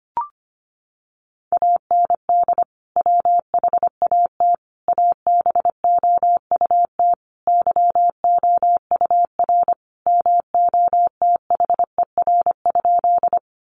Morse code, heard as a keyed tone, about 25 wpm, 700 Hz. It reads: AND WHAT ABOUT YOUR MOTHER?
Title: Sentences from Top 200 Words 25wpm